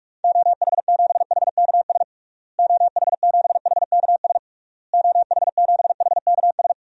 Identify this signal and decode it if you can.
Morse code, heard as a keyed tone, about 46 wpm, 700 Hz. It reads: OH7HKS OH7HKS OH7HKS
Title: DXCC-100 Call Signs - 6 Characters - Repeated 3x - 45wpm